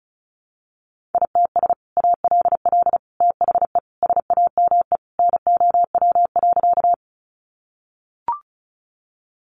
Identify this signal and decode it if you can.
Morse code, heard as a keyed tone, about 34 wpm, 700 Hz. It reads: ITS ALL THE SAME NOW.
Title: Sentences from Top 600 Words 35wpm